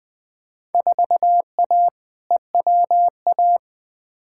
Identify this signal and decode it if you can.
Morse code, heard as a keyed tone, about 20 wpm, 700 Hz. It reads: 4A EWA
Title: Field Day Exchanges - 20wpm